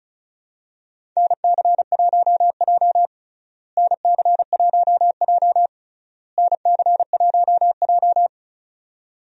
Morse code, heard as a keyed tone, about 36 wpm, 700 Hz. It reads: NC1J NC1J NC1J
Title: USA Call Signs - 4 Characters - Repeated 3x - 35wpm